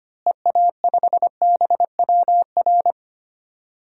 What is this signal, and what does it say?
Morse code, heard as a keyed tone, about 25 wpm, 700 Hz. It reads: EA5BWR
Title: DXCC-100 Call Signs - 6 Characters - Repeated 3x - 25wpm